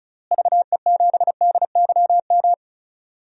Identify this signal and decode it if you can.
Morse code, heard as a keyed tone, about 36 wpm, 700 Hz. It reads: VE7DYM
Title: DXCC-100 Call Signs - 6 Characters - Repeated 2x - 35wpm